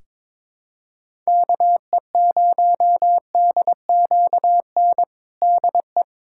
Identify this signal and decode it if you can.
Morse code, heard as a keyed tone, about 22 wpm, 700 Hz. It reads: KE0DQN DE
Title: USA Callsign DE Callsign - 6 Characters - 22wpm